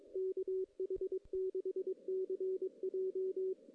Morse code, heard as a keyed tone, about 22 wpm, 380 Hz. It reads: KH6CJ
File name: 2016 JAN CW NAQP / 2016 Jan naqp k5ot - 0-6